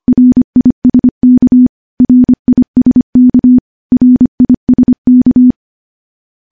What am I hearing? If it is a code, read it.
Morse code, heard as a keyed tone, about 25 wpm, 250 Hz. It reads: RISK RISK RISK
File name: morse / morse-training-25WPM